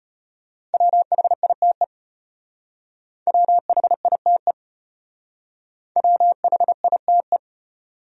Morse code, heard as a keyed tone, about 30 wpm, 700 Hz. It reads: WHITE WHITE WHITE
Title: Speed-Racing Top 500 Words 25wpm